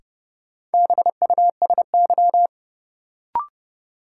Morse code, heard as a keyed tone, about 30 wpm, 700 Hz. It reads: BUSY
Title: Sentences from Top 1000 Words - Review Part 2 - New Word Drill - 30wpm